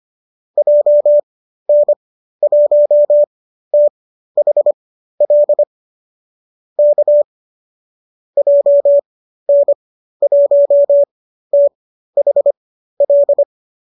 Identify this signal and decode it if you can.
Morse code, heard as a keyed tone, about 25 wpm, spaced as for 13 wpm, 590 Hz. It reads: JN1THL K JN1THL